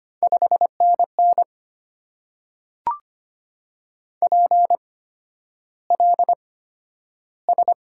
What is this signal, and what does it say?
Morse code, heard as a keyed tone, about 25 wpm, spaced as for 11 wpm, 690 Hz. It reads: 5NN PLS
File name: Top 90 QSO Elements - 8x Character Spacing - 25wpm